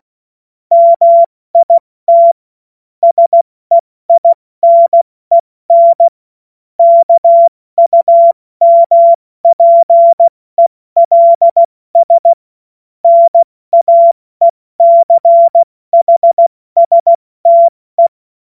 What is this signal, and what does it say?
Morse code, heard as a keyed tone, about 16 wpm, 680 Hz. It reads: MIT SEINEN KUMPELS NAECHSTE
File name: Morsekurs / Witze / Witz 19. Oktober 2021, 70 F80